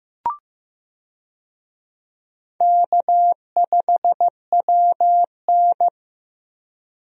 Morse code, heard as a keyed tone, about 16 wpm, 700 Hz. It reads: K5WN
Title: USA Call Signs - 4 Characters - Repeated 3x - 15wpm